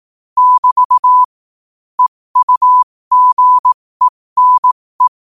Morse code, heard as a keyed tone, about 18 wpm, spaced as for 15 wpm, 1000 Hz. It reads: <BT> EUGENE
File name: qotdfiles / 15wpm